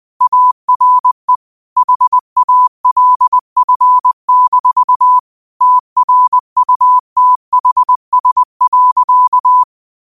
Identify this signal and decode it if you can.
Morse code, heard as a keyed tone, about 20 wpm, 1000 Hz. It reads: ARE HALF-TRUTHS.
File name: qotdfiles / 20wpm